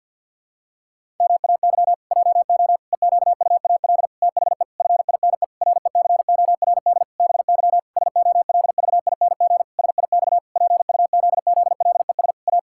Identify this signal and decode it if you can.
Morse code, heard as a keyed tone, about 51 wpm, 700 Hz. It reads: MAY JO EQUAL THE FINE RECORD BY SOLVING SIX PUZZLES A
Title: Pangram Challenge v2 - 50wpm